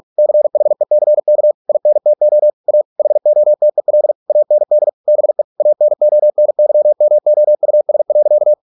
Morse code, heard as a keyed tone, about 46 wpm, 600 Hz. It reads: CHECK INTO A HOTEL AND BE ANONYMOUS.